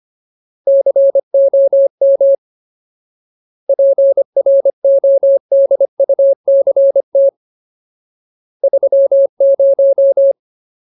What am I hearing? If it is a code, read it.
Morse code, heard as a keyed tone, about 25 wpm, 550 Hz. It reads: COM PRODUCT 30
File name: Ultimate Word List / 25WPM Ultimate